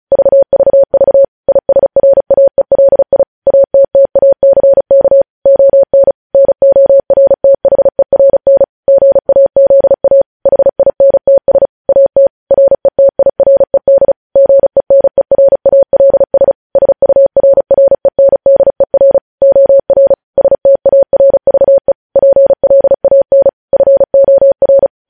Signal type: Morse code